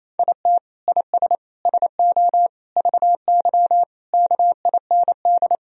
Morse code, heard as a keyed tone, about 28 wpm, 700 Hz. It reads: IT IS SO VY KIND
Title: Sentences from Top 300 Words and Abbreviations 28wpm